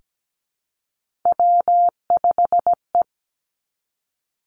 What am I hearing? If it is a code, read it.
Morse code, heard as a keyed tone, about 17 wpm, 700 Hz. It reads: W5E
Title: Three Challenging Letter-Number Combinations 17wpm